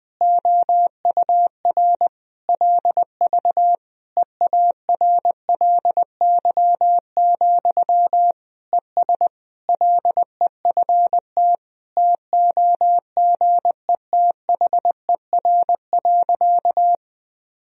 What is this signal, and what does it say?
Morse code, heard as a keyed tone, about 21 wpm, 700 Hz. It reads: OUR LV EARLY, ES LEFT TOGETHER.
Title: Sentences from Top 400 Words and Abbreviations 20wpm